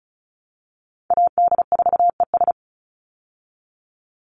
Morse code, heard as a keyed tone, about 34 wpm, 700 Hz. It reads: AD4ES
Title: Call Signs Repeated 3x - 35wpm